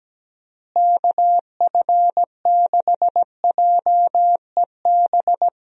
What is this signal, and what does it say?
Morse code, heard as a keyed tone, about 17 wpm, 700 Hz. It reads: KF6JEB
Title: USA Call Signs - 6 Characters - Repeated 2x - 17wpm